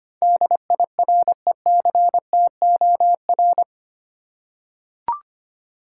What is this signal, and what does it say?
Morse code, heard as a keyed tone, about 25 wpm, 700 Hz. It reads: DIRECTOR
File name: Eight Letter Words - 25wpm